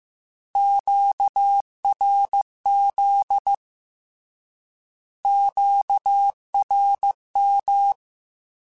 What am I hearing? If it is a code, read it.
Morse code, heard as a keyed tone, about 15 wpm, 800 Hz. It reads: QRZ QRM
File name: Short Runs / Short Runs-mp3s / Words @13